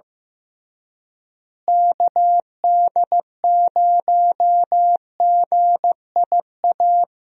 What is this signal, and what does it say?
Morse code, heard as a keyed tone, about 15 wpm, 700 Hz. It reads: KD0GIA
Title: USA Call Signs - 6 Characters - Repeated 3x - 15wpm